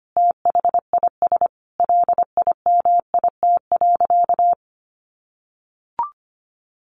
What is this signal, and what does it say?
Morse code, heard as a keyed tone, about 25 wpm, 700 Hz. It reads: THIS LIMIT.
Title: Sentences from Top 1500 Words 25wpm